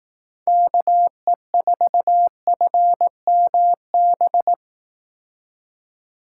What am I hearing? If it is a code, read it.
Morse code, heard as a keyed tone, about 19 wpm, 700 Hz. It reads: KE4FMB